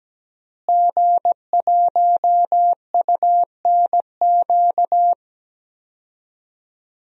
Morse code, heard as a keyed tone, about 17 wpm, 700 Hz. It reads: G1UNQ
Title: DXCC-100 Call Signs - 5 Characters - Repeated 3x - 17wpm